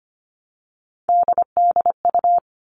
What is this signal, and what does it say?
Morse code, heard as a keyed tone, about 25 wpm, 700 Hz. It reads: DDU
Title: All Cut Numbers - 3-Digit 25wpm